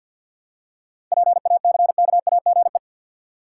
Morse code, heard as a keyed tone, about 52 wpm, 700 Hz. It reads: PACKAGE
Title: Seven Letter Words - 50wpm